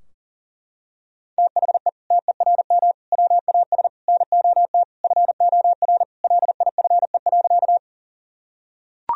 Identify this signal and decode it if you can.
Morse code, heard as a keyed tone, about 40 wpm, 700 Hz. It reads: THE TERM WAS NOT FOR LIFE.
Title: Sentences from Top 1000 Words - Review Part 4 - 40wpm